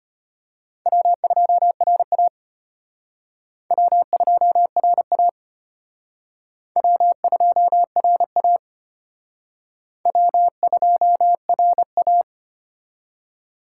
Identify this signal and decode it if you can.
Morse code, heard as a keyed tone, about 32 wpm, 700 Hz. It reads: W2RA W2RA W2RA W2RA